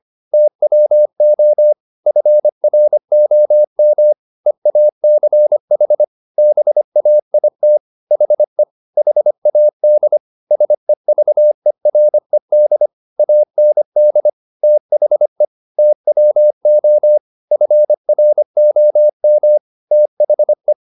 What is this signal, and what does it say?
Morse code, heard as a keyed tone, about 25 wpm, 600 Hz. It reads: TWO FROM EACH BAIT HE HAD SEVERED AND THE TWO FROM THE
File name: the old man and the sea part 1 25wpm